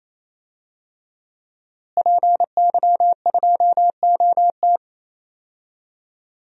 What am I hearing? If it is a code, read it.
Morse code, heard as a keyed tone, about 28 wpm, 700 Hz. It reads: PY2OT
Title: DXCC-100 Call Signs - 5 Characters - Repeated 3x - 28wpm